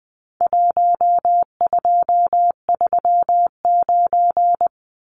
Morse code, heard as a keyed tone, about 20 wpm, 700 Hz. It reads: 1239
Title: Numbers - 4-Digit 20wpm